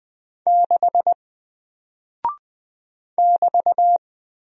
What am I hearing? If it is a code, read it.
Morse code, heard as a keyed tone, about 20 wpm, 700 Hz. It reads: 6 <BT>
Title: Single Character (Letter:Number:Punctuation:Prosign) Alt 20wpm